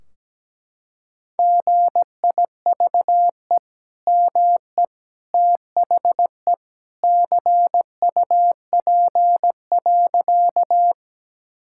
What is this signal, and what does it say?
Morse code, heard as a keyed tone, about 17 wpm, 700 Hz. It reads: GIVE ME THE CUP.
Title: Sentences from Top 1600 Words 17wpm